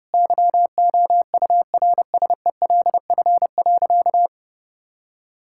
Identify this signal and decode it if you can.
Morse code, heard as a keyed tone, about 30 wpm, 700 Hz. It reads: YOURSELF.
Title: Sentences from Top 1600 Words 30wpm